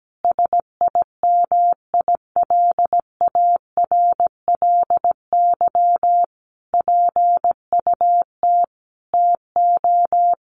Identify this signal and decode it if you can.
Morse code, heard as a keyed tone, about 17 wpm, 700 Hz. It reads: SIMILARLY PUT TO
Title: Sentences from Top 1400 Words 17wpm